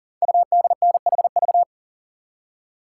Morse code, heard as a keyed tone, about 38 wpm, 700 Hz. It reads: UDNHV
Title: Code Groups - 5 Letters - 40wpm